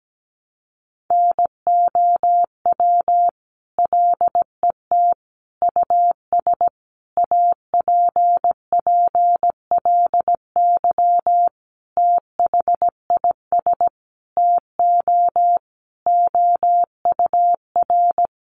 Morse code, heard as a keyed tone, about 17 wpm, 700 Hz. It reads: NOW LET US APPLY THIS TO OUR
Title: Sentences from Top 1200 Words 17wpm